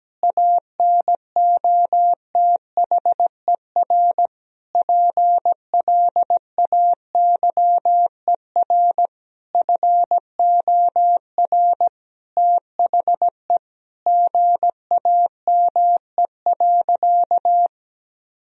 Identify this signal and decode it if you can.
Morse code, heard as a keyed tone, about 17 wpm, 700 Hz. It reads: ANOTHER PLAYER FOR THE GAME.
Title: Sentences from Top 1000 Words - Review Part 4 - 17wpm